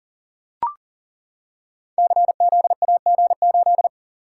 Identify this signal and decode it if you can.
Morse code, heard as a keyed tone, about 41 wpm, 700 Hz. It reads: CZAG8
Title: Code Groups - 5 Letter:Numbers - 40wpm